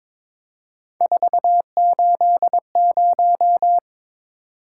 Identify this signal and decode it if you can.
Morse code, heard as a keyed tone, about 22 wpm, 700 Hz. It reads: 480
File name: practice / 22wpm / Three Letter-Number Combinations - 22wpm